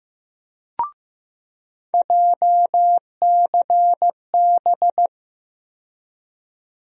Morse code, heard as a keyed tone, about 15 wpm, 700 Hz. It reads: JCB
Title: USA Call Sign Postfixes - 3 Letters 15wpm